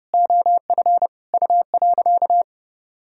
Morse code, heard as a keyed tone, about 30 wpm, 700 Hz. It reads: OF U.
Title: Sentences from Top 200 Words and Abbreviations 30wpm